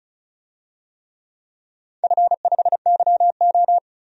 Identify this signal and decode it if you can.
Morse code, heard as a keyed tone, about 36 wpm, 700 Hz. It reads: F5YO
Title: DXCC-100 Call Signs - 4 Characters - 35wpm